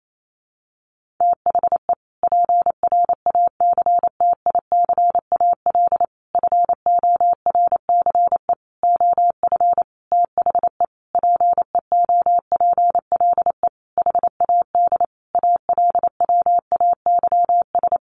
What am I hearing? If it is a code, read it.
Morse code, heard as a keyed tone, about 28 wpm, 700 Hz. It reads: THE PRACTICAL FORCE OF THE PEOPLE HAD ALWAYS
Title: Sentences from Top 1400 Words 28wpm